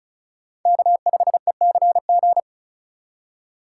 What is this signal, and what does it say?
Morse code, heard as a keyed tone, about 34 wpm, 700 Hz. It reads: K5ECG
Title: USA Call Signs - 5 Characters - 8x Character Spacing - 35wpm